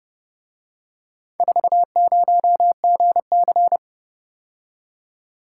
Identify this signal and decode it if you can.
Morse code, heard as a keyed tone, about 29 wpm, 700 Hz. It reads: 40GC